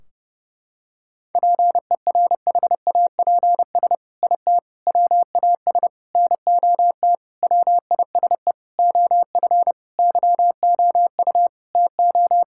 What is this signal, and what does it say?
Morse code, heard as a keyed tone, about 30 wpm, 700 Hz. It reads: PERHAPS IT WAS NOT WISE OF YOU TO
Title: Sentences from Top 1600 Words 30wpm